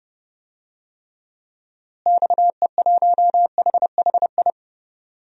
Morse code, heard as a keyed tone, about 32 wpm, 700 Hz. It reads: XE1HHI
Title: DXCC-100 Call Signs - 6 Characters - Repeated 3x - 30wpm